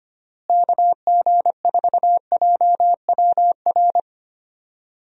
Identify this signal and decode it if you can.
Morse code, heard as a keyed tone, about 25 wpm, 700 Hz. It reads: KG4JWR
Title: USA Call Signs - 6 Characters - Repeated 2x - 25wpm